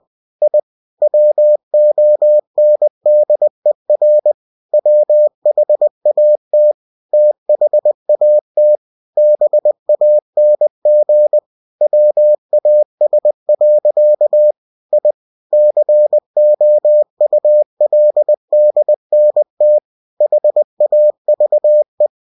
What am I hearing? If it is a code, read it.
Morse code, heard as a keyed tone, about 20 wpm, 600 Hz. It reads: I WONDER WHAT THAT BANG WAS. I COULDNT HAVE
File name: cw-books / winnie-the-pooh / winnie-20wpm-page-0053